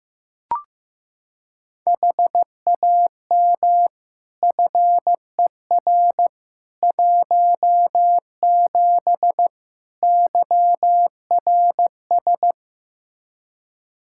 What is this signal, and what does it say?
Morse code, heard as a keyed tone, about 15 wpm, 700 Hz. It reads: HAM FER 17 YRS
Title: Practice QSO - Exchange 3 - More Info - 15wpm